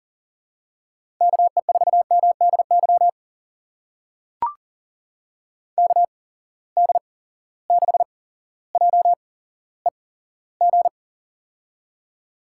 Morse code, heard as a keyed tone, about 42 wpm, spaced as for 17 wpm, 700 Hz. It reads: KE4MDY KD6JEG
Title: USA Call Signs - 6 Characters - 8x Character Spacing - 40wpm